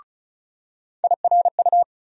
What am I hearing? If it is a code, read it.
Morse code, heard as a keyed tone, about 36 wpm, 700 Hz. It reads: IRU
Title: USA Call Sign Postfixes - 3 Letters 35wpm